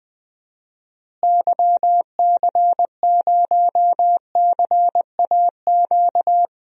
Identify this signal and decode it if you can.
Morse code, heard as a keyed tone, about 20 wpm, 700 Hz. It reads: YC0CAQ
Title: DXCC-100 Call Signs - 6 Characters - Repeated 2x - 20wpm